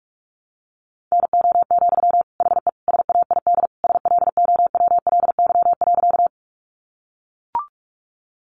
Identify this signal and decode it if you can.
Morse code, heard as a keyed tone, about 44 wpm, 700 Hz. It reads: NO, HE SAID SLOWLY.
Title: Sentences from Top 1100 Words 45wpm